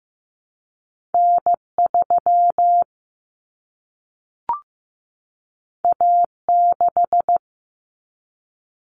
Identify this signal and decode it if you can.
Morse code, heard as a keyed tone, about 15 wpm, 700 Hz. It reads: N3 A6
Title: Common Cut Numbers - 2-Digit 15wpm